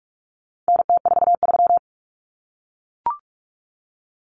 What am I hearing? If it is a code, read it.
Morse code, heard as a keyed tone, about 44 wpm, 700 Hz. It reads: NT43
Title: Common Cut Numbers - 4-Digit 45wpm